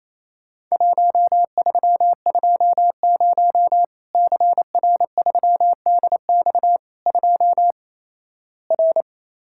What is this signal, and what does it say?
Morse code, heard as a keyed tone, about 28 wpm, 700 Hz. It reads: 1320 CR3DX 2 R
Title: Worked All Europe DX Contest - QTC Exchange - 28wpm